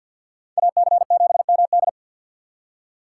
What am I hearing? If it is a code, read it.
Morse code, heard as a keyed tone, about 59 wpm, 700 Hz. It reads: AC7MD
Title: Speed Racing - USA 2x2 Call Signs 50wpm